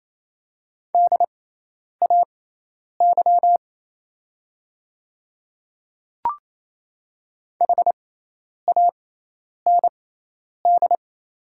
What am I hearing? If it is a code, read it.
Morse code, heard as a keyed tone, about 28 wpm, spaced as for 10 wpm, 700 Hz. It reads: DAY HAND
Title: 6x Character Spacing - Top 200 Words - 28wpm